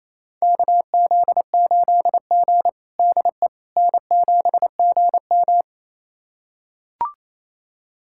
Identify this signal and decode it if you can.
Morse code, heard as a keyed tone, about 28 wpm, 700 Hz. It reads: KZ8G DE N7GM
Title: USA Callsign DE Callsign - 4 Characters - 28wpm